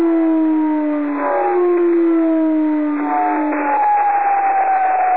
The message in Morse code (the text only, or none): none